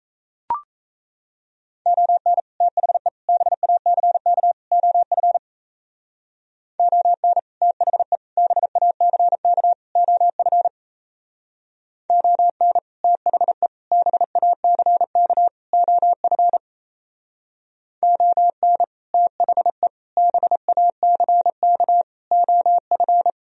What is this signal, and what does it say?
Morse code, heard as a keyed tone, about 34 wpm, 700 Hz. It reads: ON THE BACK OF ON THE BACK OF ON THE BACK OF ON THE BACK OF
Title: Speed-Racing - Sets of 4 Words 28wpm